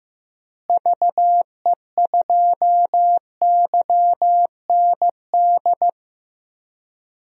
Morse code, heard as a keyed tone, about 15 wpm, 700 Hz. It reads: VE2YND
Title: DXCC-100 Call Signs - 6 Characters - Repeated 3x - 15wpm